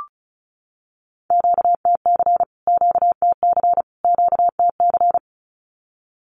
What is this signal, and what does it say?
Morse code, heard as a keyed tone, about 34 wpm, 700 Hz. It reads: QTC QTC QTC
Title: Q-Codes Repeated 3x 35wpm